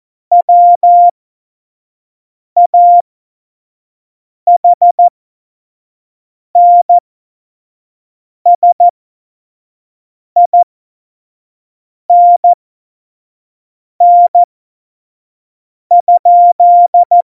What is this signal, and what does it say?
Morse code, heard as a keyed tone, about 14 wpm, spaced as for 5 wpm, 700 Hz. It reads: WAHNSINN?
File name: Morsekurs / Witze / Witz 25 Juli 17, Tempo 25 F70